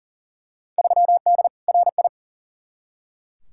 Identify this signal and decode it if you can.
Morse code, heard as a keyed tone, about 37 wpm, 700 Hz. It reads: 3D RI
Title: Field Day Exchanges Encore - 40wpm